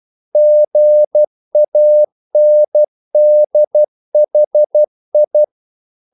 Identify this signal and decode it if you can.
Morse code, heard as a keyed tone, about 12 wpm, 600 Hz. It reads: GANDHI
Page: English quotations in CW (telegraphy - morse)